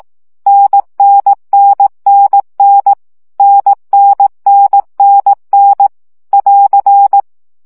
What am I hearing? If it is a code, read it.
Morse code, heard as a keyed tone, about 18 wpm, 800 Hz. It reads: NNNNN NNNNN <AR>
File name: CW gyakorlas n 15 baud